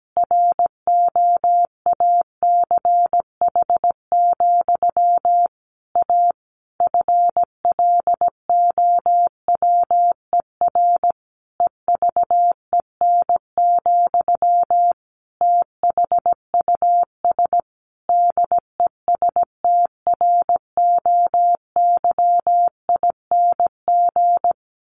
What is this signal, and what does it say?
Morse code, heard as a keyed tone, about 17 wpm, 700 Hz. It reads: ROACH, A FLOWER EVEN, THUS DESTROYING
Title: A Sound Of Thunder 17wpm